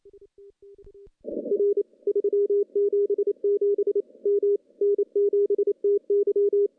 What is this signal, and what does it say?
Morse code, heard as a keyed tone, about 28 wpm, 410 Hz. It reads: R 377 M N7TY
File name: CW-SS-k5tr